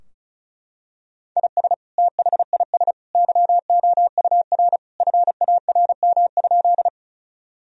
Morse code, heard as a keyed tone, about 34 wpm, 700 Hz. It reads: IS THIS YOUR FARM?
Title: Sentences from Top 1000 Words - Review Part 2 - 35wpm